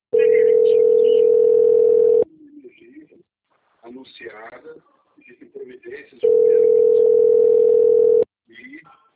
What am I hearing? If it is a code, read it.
Ringback tone (North America), 440 + 480 Hz, with speech.
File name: out- - - - - .582